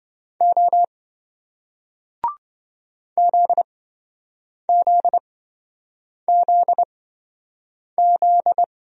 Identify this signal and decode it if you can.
Morse code, heard as a keyed tone, about 27 wpm, spaced as for 7 wpm, 700 Hz. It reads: O ZZZZ